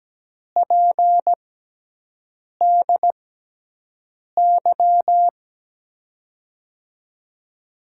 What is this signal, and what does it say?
Morse code, heard as a keyed tone, about 17 wpm, spaced as for 6 wpm, 700 Hz. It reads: PDY
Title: USA Call Signs - 5 Characters - 6x Character Spacing - 17wpm